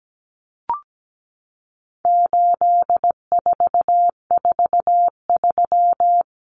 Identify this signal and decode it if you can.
Morse code, heard as a keyed tone, about 17 wpm, 700 Hz. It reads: 8443
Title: Common Cut Numbers - 4-Digit 17wpm